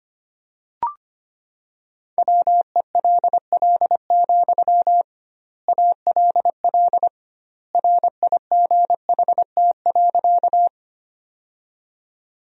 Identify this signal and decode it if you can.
Morse code, heard as a keyed tone, about 25 wpm, 700 Hz. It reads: WELL, ALL RIGHT.
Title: 2x Word-Spacing - Sentences from Top 200 Words - 25wpm